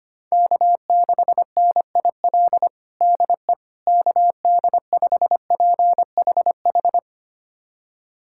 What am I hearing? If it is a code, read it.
Morse code, heard as a keyed tone, about 25 wpm, 700 Hz. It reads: K6NIL DE KD5PHH
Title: USA Callsign DE Callsign 25wpm